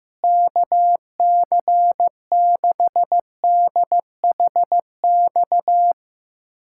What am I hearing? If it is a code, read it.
Morse code, heard as a keyed tone, about 15 wpm, 700 Hz. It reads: KC6DHX
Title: USA Callsign DE Callsign 15wpm